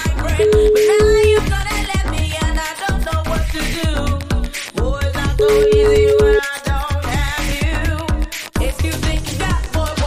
Ringback tone (Europe, 425 Hz), with noise.